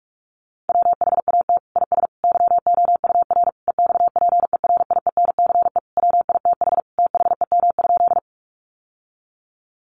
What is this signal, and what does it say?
Morse code, heard as a keyed tone, about 44 wpm, 700 Hz. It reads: WHAT IS YOUR EXPERIENCE WITH THEM?
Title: Sentences from Top 600 Words 45wpm